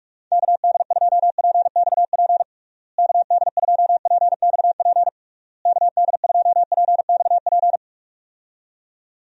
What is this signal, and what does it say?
Morse code, heard as a keyed tone, about 41 wpm, 700 Hz. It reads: KD2PXP KD2PXP KD2PXP
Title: USA Call Signs - 6 Characters - Repeated 3x - 45wpm